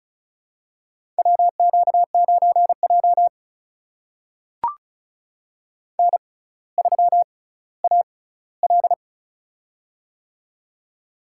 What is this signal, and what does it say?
Morse code, heard as a keyed tone, about 36 wpm, spaced as for 18 wpm, 700 Hz. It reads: WQ9J N3AL
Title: USA Call Signs - 6x Character Spacing - 35wpm